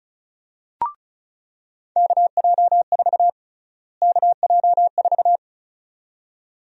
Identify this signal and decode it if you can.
Morse code, heard as a keyed tone, about 36 wpm, 700 Hz. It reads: KJ4 KJ4
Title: USA Call Sign Prefixes - Repeated 2x - 35wpm